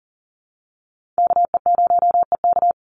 Morse code, heard as a keyed tone, about 40 wpm, 700 Hz. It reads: KE0EK
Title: Call Signs - 40wpm